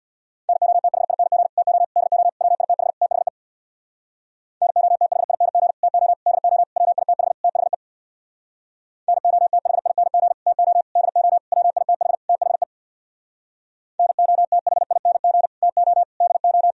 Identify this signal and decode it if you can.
Morse code, heard as a keyed tone, about 64 wpm, 690 Hz. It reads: NOTHING TO DO WITH THE NOTHING TO DO WITH THE NOTHING TO DO WITH THE NOTHING TO DO